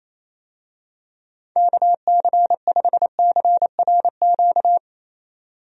Morse code, heard as a keyed tone, about 28 wpm, 700 Hz. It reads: KC5CRQ
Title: USA Call Signs - 6 Characters - 28wpm